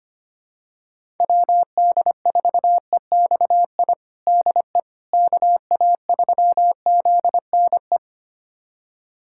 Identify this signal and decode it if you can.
Morse code, heard as a keyed tone, about 25 wpm, 700 Hz. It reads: WD4EXI DE KA3ZNE
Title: USA Callsign DE Callsign - 6 Characters - 25wpm